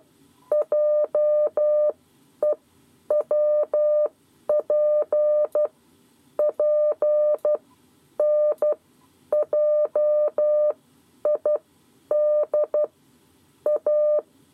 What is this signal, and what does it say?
Morse code, heard as a keyed tone, about 12 wpm, spaced as for 9 wpm, 600 Hz. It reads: JEWPP NJIDA
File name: 7 Lezione cw 2